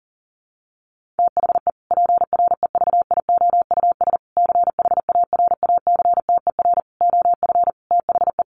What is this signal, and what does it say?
Morse code, heard as a keyed tone, about 40 wpm, 700 Hz. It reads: THE PREVIOUS CHARACTER OF THE
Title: Sentences from Top 1500 Words 40wpm